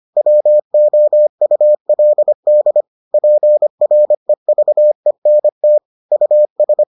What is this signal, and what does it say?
Morse code, heard as a keyed tone, about 25 wpm, 600 Hz. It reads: WOULD PREVENT US